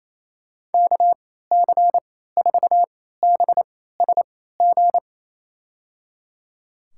Morse code, heard as a keyed tone, about 28 wpm, spaced as for 16 wpm, 700 Hz. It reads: KC4BSG
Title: USA Call Signs - 6 Characters - 3x Character Spacing - 28wpm